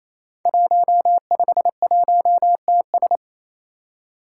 Morse code, heard as a keyed tone, about 28 wpm, 700 Hz. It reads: 151TS
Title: Five Challenging Letter-Number Combinations 28wpm